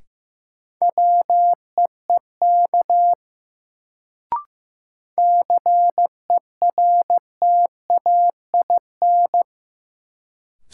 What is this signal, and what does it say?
Morse code, heard as a keyed tone, about 15 wpm, 700 Hz. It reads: WEEK CERTAIN
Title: Sentences from Top 500 Words - New Word Drill - 15wpm